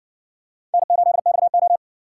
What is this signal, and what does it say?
Morse code, heard as a keyed tone, about 60 wpm, 700 Hz. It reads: N8XO